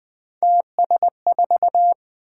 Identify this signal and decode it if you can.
Morse code, heard as a keyed tone, about 20 wpm, 700 Hz. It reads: TS4
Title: Three Letter-Number Combinations - 20wpm